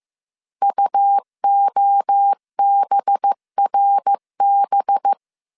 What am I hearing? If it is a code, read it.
Morse code, heard as a keyed tone, about 15 wpm, 800 Hz. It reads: UOBRB